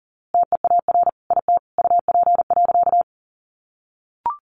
Morse code, heard as a keyed tone, about 40 wpm, 700 Hz. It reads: TEAR IT UP.
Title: A Sound Of Thunder 40wpm